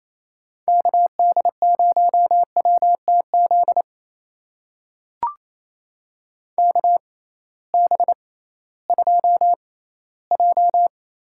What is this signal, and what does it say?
Morse code, heard as a keyed tone, about 28 wpm, spaced as for 16 wpm, 700 Hz. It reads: KD0WTZ KB2J
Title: USA Call Signs - 6 Characters - 6x Character Spacing - 28wpm